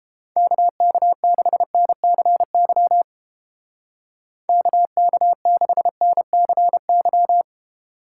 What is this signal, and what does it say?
Morse code, heard as a keyed tone, about 29 wpm, 700 Hz. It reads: KK6NCY KK6NCY